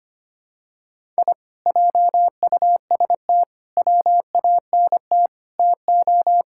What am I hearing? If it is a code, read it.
Morse code, heard as a keyed tone, about 25 wpm, 700 Hz. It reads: I JUST WANT TO
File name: practice / 25wpm / Sentences with 4-Word Phrases - 25wpm